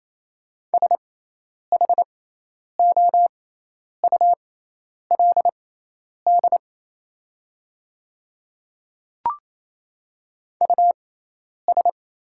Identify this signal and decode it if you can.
Morse code, heard as a keyed tone, about 28 wpm, spaced as for 10 wpm, 690 Hz. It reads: SHOULD US
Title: 6x Character Spacing - Top 300 Words - 28wpm